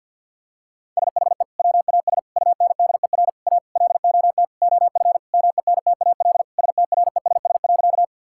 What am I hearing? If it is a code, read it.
Morse code, heard as a keyed tone, about 48 wpm, 700 Hz. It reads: SHE WAS UNDER A LOT OF MENTAL STRESS.
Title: Sentences from Top 1000 Words - Review Part 6 - 50wpm